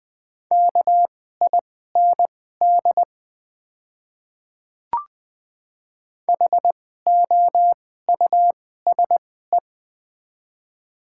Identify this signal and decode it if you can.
Morse code, heard as a keyed tone, about 20 wpm, spaced as for 14 wpm, 700 Hz. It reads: KIND HOUSE
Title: 2x Character Spacing - Top 200 Words - 20wpm